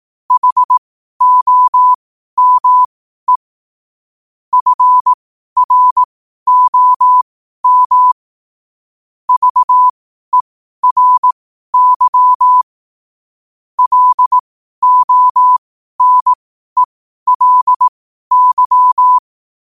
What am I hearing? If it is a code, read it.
Morse code, heard as a keyed tone, about 18 wpm, spaced as for 12 wpm, 1000 Hz. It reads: HOME FROM VERY LONELY